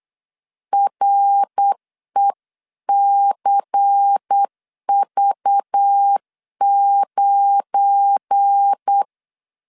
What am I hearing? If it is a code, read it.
Morse code, heard as a keyed tone, about 8 wpm, 800 Hz. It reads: RECV9
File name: CW / DET / 1 9 1500C S B 8wpm 5624